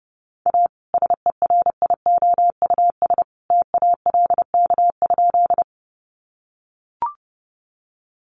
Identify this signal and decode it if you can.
Morse code, heard as a keyed tone, about 30 wpm, 700 Hz. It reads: A SERIOUS TALK?
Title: Sentences from Top 1100 Words 30wpm